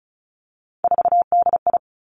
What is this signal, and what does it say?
Morse code, heard as a keyed tone, about 36 wpm, 700 Hz. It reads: 4DI
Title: Three Challenging Letter-Number Combinations 35wpm